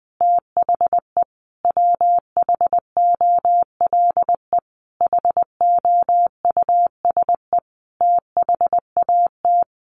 Morse code, heard as a keyed tone, about 20 wpm, 700 Hz. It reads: THE WHOLE HOUSE THAT
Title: Sentences from Top 1200 Words 20wpm